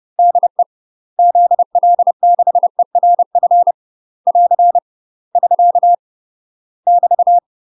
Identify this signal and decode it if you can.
Morse code, heard as a keyed tone, about 30 wpm, 700 Hz. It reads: DE ZL6ERF <AR> <SK> <BT>
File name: UK MASTER QSOs TRAINING / 30WPM / UK MASTER QSOs 0027